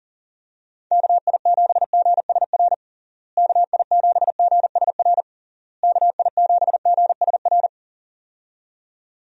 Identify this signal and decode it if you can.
Morse code, heard as a keyed tone, about 37 wpm, 700 Hz. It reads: KI7GSR KI7GSR KI7GSR
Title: USA Call Signs - 6 Characters - Repeated 3x - 40wpm